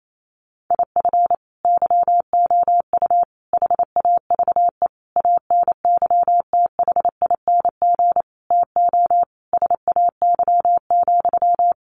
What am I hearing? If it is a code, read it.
Morse code, heard as a keyed tone, about 28 wpm, 700 Hz. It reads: IF YOU HAVE ANYTHING TO SAY,
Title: Sentences from Top 1000 Words 28wpm